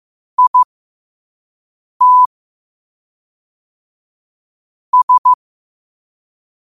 Morse code, heard as a keyed tone, about 15 wpm, spaced as for 5 wpm, 1000 Hz. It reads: IT S
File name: qotdfiles / 5wpm